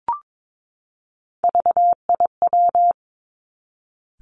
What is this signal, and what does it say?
Morse code, heard as a keyed tone, about 22 wpm, 700 Hz. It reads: VIW
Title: Three Challenging Letter-Number Combinations 22wpm